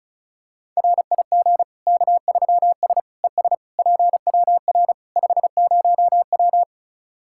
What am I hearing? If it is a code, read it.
Morse code, heard as a keyed tone, about 36 wpm, 700 Hz. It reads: RIG K3S ES PWR 50W
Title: Practice QSO - Exchange 2 - Info - 35wpm